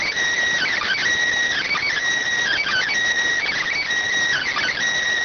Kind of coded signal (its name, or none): Emergency Alert System (SAME) tones